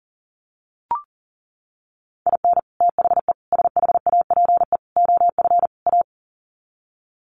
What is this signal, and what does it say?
Morse code, heard as a keyed tone, about 40 wpm, 700 Hz. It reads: IN THE SHAPE OF A
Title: Sets of 5 Words Encore 40wpm